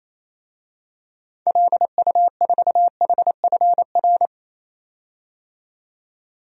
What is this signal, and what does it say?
Morse code, heard as a keyed tone, about 27 wpm, 700 Hz. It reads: LU4HFR
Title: DXCC-100 Call Signs - 6 Characters - 28wpm